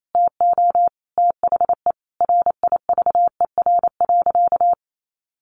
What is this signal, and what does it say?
Morse code, heard as a keyed tone, about 28 wpm, 700 Hz. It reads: TO THE RIVER.
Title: Sentences from Top 1400 Words 28wpm